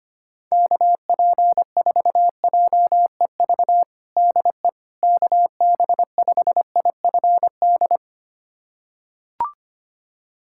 Morse code, heard as a keyed tone, about 25 wpm, 700 Hz. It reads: KP4JEV DE KB5IFD
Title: USA Callsign DE Callsign 25wpm